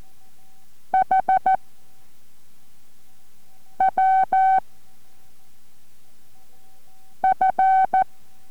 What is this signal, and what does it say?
Morse code, heard as a keyed tone, about 14 wpm, 760 Hz. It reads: H W F